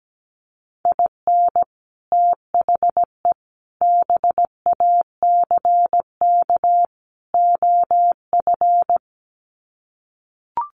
Morse code, heard as a keyed tone, about 17 wpm, 700 Hz. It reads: IN THE BACK OF
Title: Sets of 4 Words 17wpm